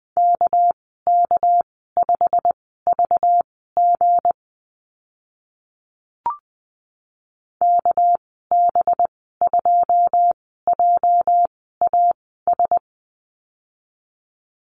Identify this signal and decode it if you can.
Morse code, heard as a keyed tone, about 20 wpm, spaced as for 14 wpm, 700 Hz. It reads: KK5VG KB2JAS
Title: 213 Lesson 34g - Number 8 - Call Signs 20wpm